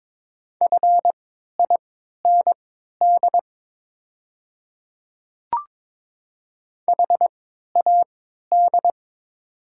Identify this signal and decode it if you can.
Morse code, heard as a keyed tone, about 23 wpm, spaced as for 13 wpm, 700 Hz. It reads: FIND HAD
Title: 3x Character Spacing - Top 300 Words - 22wpm